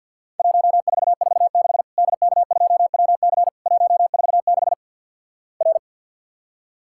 Morse code, heard as a keyed tone, about 50 wpm, 700 Hz. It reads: 1446 DK2WC 146 R
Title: Worked All Europe DX Contest - QTC Exchange - 50wpm